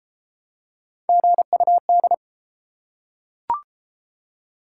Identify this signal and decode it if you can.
Morse code, heard as a keyed tone, about 28 wpm, 700 Hz. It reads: GUD